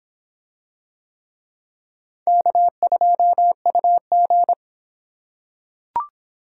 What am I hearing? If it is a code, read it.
Morse code, heard as a keyed tone, about 23 wpm, 700 Hz. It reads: K2UG